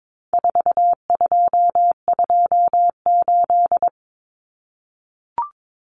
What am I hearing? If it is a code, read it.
Morse code, heard as a keyed tone, about 22 wpm, 700 Hz. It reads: 4228
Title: Numbers - 4-Digit 22wpm